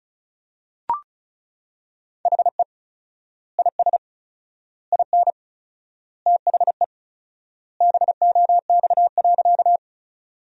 Morse code, heard as a keyed tone, about 36 wpm, 700 Hz. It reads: HE IS IN THE BOX.
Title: 4x Word-Spacing - Sentences from Top 500 Words - 35wpm